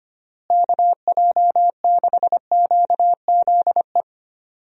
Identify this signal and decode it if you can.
Morse code, heard as a keyed tone, about 25 wpm, 700 Hz. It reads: KJ6QZE
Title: USA Call Signs - 6 Characters - Repeated 2x - 25wpm